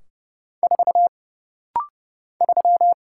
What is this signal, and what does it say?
Morse code, heard as a keyed tone, about 30 wpm, 700 Hz. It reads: 4 3
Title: Single Letter-Number - Rapid-Fire 30wpm